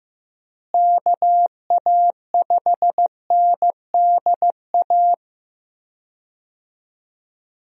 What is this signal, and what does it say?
Morse code, heard as a keyed tone, about 15 wpm, 700 Hz. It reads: KA5NDA
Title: USA Call Signs - 6 Characters - Repeated 3x - 15wpm